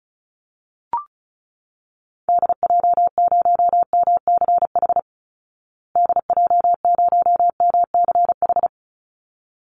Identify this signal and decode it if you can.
Morse code, heard as a keyed tone, about 34 wpm, 700 Hz. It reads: DJ0MCH DJ0MCH
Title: Call Signs Repeated 3x - 35wpm